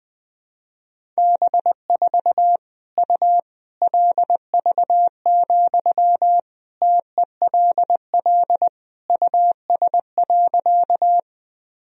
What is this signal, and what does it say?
Morse code, heard as a keyed tone, about 21 wpm, 700 Hz. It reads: B4 U LV, TELL US.
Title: Sentences from Top 400 Words and Abbreviations 20wpm